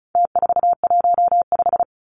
Morse code, heard as a keyed tone, about 34 wpm, 700 Hz. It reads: T415
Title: Numbers - Contest Style - 4-Digit 35wpm